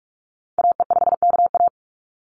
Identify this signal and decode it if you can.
Morse code, heard as a keyed tone, about 44 wpm, 700 Hz. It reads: AE5KA
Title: NA SOTA Activator Callsigns 45wpm